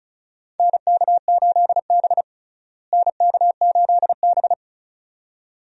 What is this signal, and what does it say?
Morse code, heard as a keyed tone, about 36 wpm, 700 Hz. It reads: NK8B NK8B
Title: USA Call Signs - 4 Characters - Repeated 2x - 35wpm